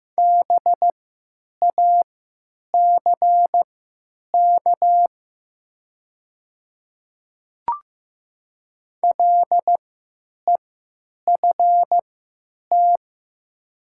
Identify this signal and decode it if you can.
Morse code, heard as a keyed tone, about 15 wpm, spaced as for 9 wpm, 700 Hz. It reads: BACK LEFT
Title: 3x Character Spacing - Top 300 Words - 15wpm